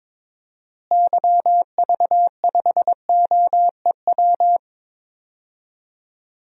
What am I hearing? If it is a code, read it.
Morse code, heard as a keyed tone, about 22 wpm, 700 Hz. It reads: YV5OEW
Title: DXCC-100 Call Signs - 6 Characters - Repeated 2x - 22wpm